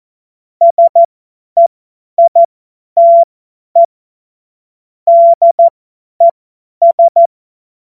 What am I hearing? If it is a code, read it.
Morse code, heard as a keyed tone, about 14 wpm, spaced as for 10 wpm, 680 Hz. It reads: SEITE DES
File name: Morsekurs / Witze / Witz 07.November 2023 50 F70